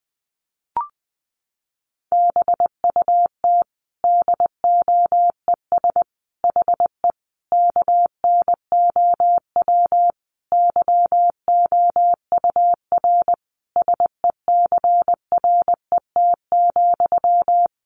Morse code, heard as a keyed tone, about 20 wpm, 700 Hz. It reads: BUT DOES HE KNOW YOUR SECRET,
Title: Sentences from Top 1200 Words 20wpm